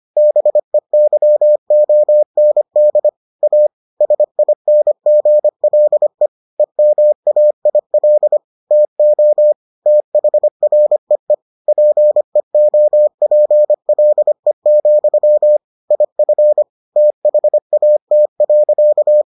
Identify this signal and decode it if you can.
Morse code, heard as a keyed tone, about 25 wpm, 600 Hz. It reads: BEYOND A SINGLE EMAIL TO THREE PEOPLE, IF THAT.